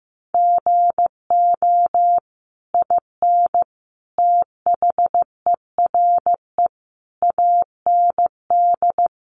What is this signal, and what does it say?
Morse code, heard as a keyed tone, about 15 wpm, 700 Hz. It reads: GO IN THERE AND
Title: Sentences from Top 500 Words 15wpm